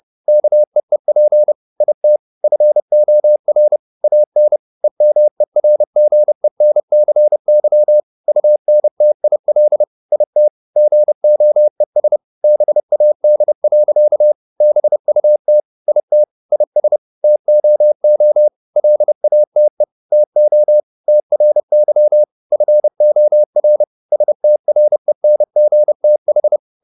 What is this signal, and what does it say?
Morse code, heard as a keyed tone, about 30 wpm, 600 Hz. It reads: KEEP IT FOR AN EMERGENCY UNTIL IT GOES BAD. BUT IT IS TOO LATE TO TRY FOR STRENGTH